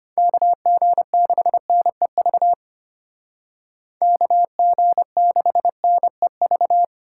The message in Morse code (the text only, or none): KG6NEV KG6NEV